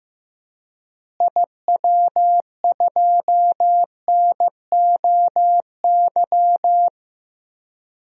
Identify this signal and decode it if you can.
Morse code, heard as a keyed tone, about 15 wpm, 700 Hz. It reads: IW2NOY
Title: DXCC-100 Call Signs - 6 Characters - Repeated 2x - 15wpm